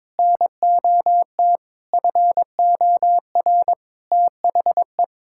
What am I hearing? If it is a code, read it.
Morse code, heard as a keyed tone, about 22 wpm, 700 Hz. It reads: NOT FOR THE
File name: practice / 22wpm / Sets of 7 Words - 22wpm